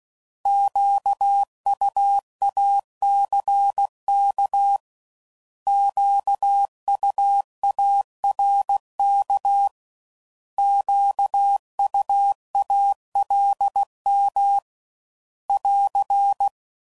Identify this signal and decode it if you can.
Morse code, heard as a keyed tone, about 16 wpm, 800 Hz. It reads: QUACK QUARK QUALM <AR>
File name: Prefixes / By First Letter / Qs